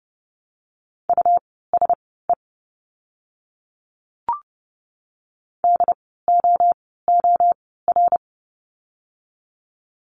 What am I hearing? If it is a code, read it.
Morse code, heard as a keyed tone, about 30 wpm, spaced as for 17 wpm, 700 Hz. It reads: USE DOOR
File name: Lesson 17f - Letter u - Words 30wpm